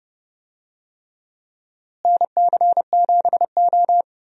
Morse code, heard as a keyed tone, about 30 wpm, 700 Hz. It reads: NC7O
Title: USA Call Signs - 4 Characters - Repeated 3x - 30wpm